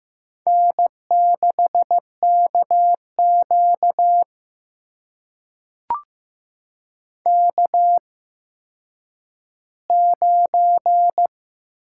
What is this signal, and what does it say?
Morse code, heard as a keyed tone, about 15 wpm, spaced as for 9 wpm, 700 Hz. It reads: N6KQ K9
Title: USA Call Signs - 4 Characters - 8x Character Spacing - 15wpm